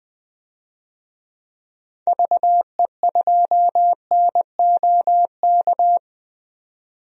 Morse code, heard as a keyed tone, about 20 wpm, 700 Hz. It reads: VE2NOK
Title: DXCC-100 Call Signs - 6 Characters - 20wpm